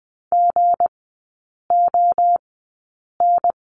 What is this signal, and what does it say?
Morse code, heard as a keyed tone, about 20 wpm, spaced as for 8 wpm, 700 Hz. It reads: GON
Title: 191 Lesson 31e - Letter v - Three Characters 20wpm